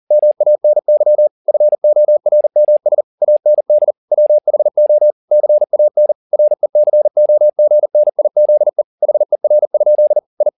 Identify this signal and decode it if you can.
Morse code, heard as a keyed tone, about 40 wpm, 600 Hz. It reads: MANY FORMS AND WHO CAN RECOGNIZE HER? I
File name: the old man and the sea 2 40wpm